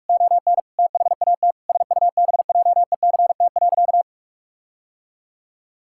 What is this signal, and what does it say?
Morse code, heard as a keyed tone, about 41 wpm, 700 Hz. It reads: ON THAT SUBJECT.
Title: Sentences from Top 1600 Words 45wpm